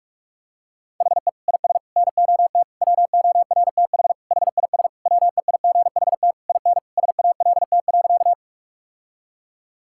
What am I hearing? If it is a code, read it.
Morse code, heard as a keyed tone, about 41 wpm, 700 Hz. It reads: HE IS NOT WORTH HIS WEIGHT IN SALT.
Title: Sentences from Top 1000 Words - Review Part 5 - 45wpm